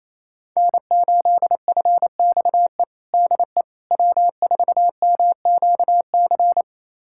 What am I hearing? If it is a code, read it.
Morse code, heard as a keyed tone, about 28 wpm, 700 Hz. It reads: N8FXE DE W4MQC
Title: USA Callsign DE Callsign 28wpm